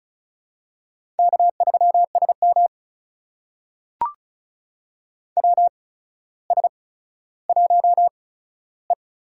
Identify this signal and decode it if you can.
Morse code, heard as a keyed tone, about 36 wpm, spaced as for 15 wpm, 700 Hz. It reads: K3SM WS1E
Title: USA Call Signs - 4 Characters - 8x Character Spacing - 35wpm